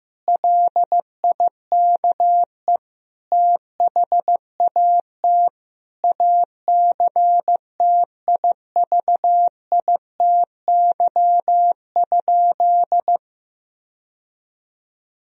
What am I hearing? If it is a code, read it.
Morse code, heard as a keyed tone, about 15 wpm, 700 Hz. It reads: LIKE THAT ACTIVITY?
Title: Sentences from Top 1000 Words - Review Part 4 - 15wpm